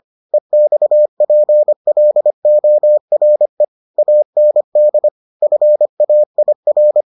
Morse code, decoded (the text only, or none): EXPLORE AND FAIR